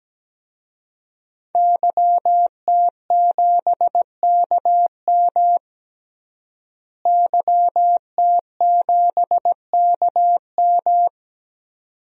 Morse code, heard as a keyed tone, about 17 wpm, 700 Hz. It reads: YT7KM YT7KM
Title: DXCC-100 Call Signs - 5 Characters - Repeated 2x - 17wpm